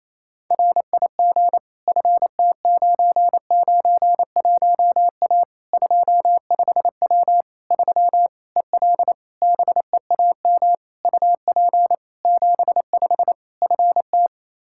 Morse code, heard as a keyed tone, about 28 wpm, 700 Hz. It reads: RIG FT991A 25W 3 EL BEAM UP 75 FT
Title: K1USN SST QSO Practice 28wpm